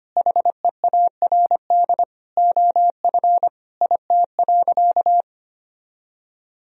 Morse code, heard as a keyed tone, about 25 wpm, 700 Hz. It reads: HEARD OF IT.
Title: Sentences from Top 1000 Words - Review Part 3 - 25wpm